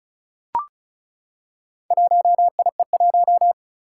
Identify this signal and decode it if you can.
Morse code, heard as a keyed tone, about 36 wpm, 700 Hz. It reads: 1IE1
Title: Four Challenging Letter-Number Combinations 35wpm